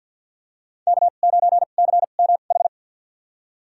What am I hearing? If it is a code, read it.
Morse code, heard as a keyed tone, about 51 wpm, spaced as for 36 wpm, 700 Hz. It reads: K9CMH
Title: USA Call Signs - 5 Characters - 2x Character Spacing - 50wpm